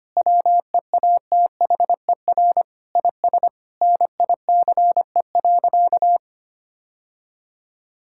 Morse code, heard as a keyed tone, about 25 wpm, 700 Hz. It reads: WEATHER IS NICE.
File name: practice / 25wpm / Difficult Sentences with Hints - 25wpm